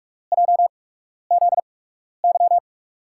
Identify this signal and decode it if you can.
Morse code, heard as a keyed tone, about 44 wpm, spaced as for 12 wpm, 700 Hz. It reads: JZY
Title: USA Call Signs - 5 Characters - 8x Character Spacing - 45wpm